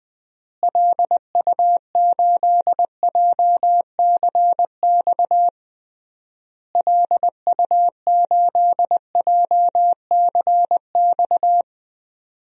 Morse code, heard as a keyed tone, about 20 wpm, 700 Hz. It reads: LU8JCX LU8JCX
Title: DXCC-100 Call Signs - 6 Characters - Repeated 3x - 20wpm